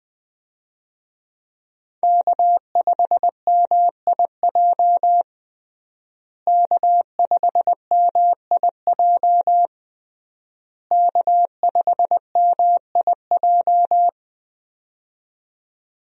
Morse code, heard as a keyed tone, about 20 wpm, 700 Hz. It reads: K5MIJ K5MIJ K5MIJ
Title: USA Call Signs - 5 Characters - Repeated 3x - 20wpm